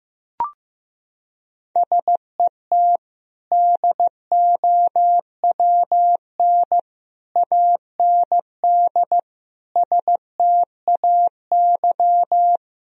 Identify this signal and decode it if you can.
Morse code, heard as a keyed tone, about 15 wpm, 700 Hz. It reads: SET DOWN AND STAY
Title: Sentences from Top 1000 Words - Review Part 1 - 15wpm